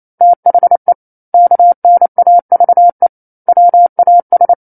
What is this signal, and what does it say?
Morse code, heard as a keyed tone, about 28 wpm, 700 Hz. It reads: THE KNAVE WAS
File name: 09-Chapter-09 28 WPM